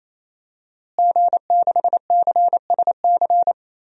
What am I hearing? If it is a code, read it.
Morse code, heard as a keyed tone, about 28 wpm, 700 Hz. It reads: G6CSC
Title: DXCC-100 Call Signs - 5 Characters - Repeated 3x - 28wpm